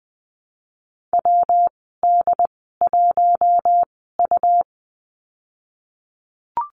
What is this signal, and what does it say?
Morse code, heard as a keyed tone, about 20 wpm, spaced as for 14 wpm, 700 Hz. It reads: WD1U
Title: 128 Lesson 21g - Number 6 - Call Signs 20wpm